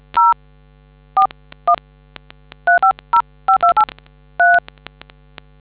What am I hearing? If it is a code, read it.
DTMF, heard as telephone touch tones, tones of about 110 ms. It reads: 0 41 3505283